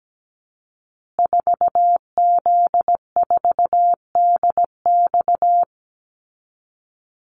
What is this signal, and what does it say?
Morse code, heard as a keyed tone, about 17 wpm, 700 Hz. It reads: 4Z4DX
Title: Call Signs Repeated 2x - 17wpm